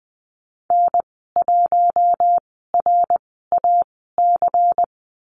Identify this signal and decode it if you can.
Morse code, heard as a keyed tone, about 20 wpm, spaced as for 15 wpm, 700 Hz. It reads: N1RAC
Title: 161 Lesson 26g - Letter g - Call Signs 20wpm